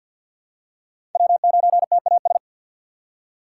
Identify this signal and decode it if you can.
Morse code, heard as a keyed tone, about 52 wpm, 700 Hz. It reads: W9TAS
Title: USA Call Signs - 4x Character Spacing - 50wpm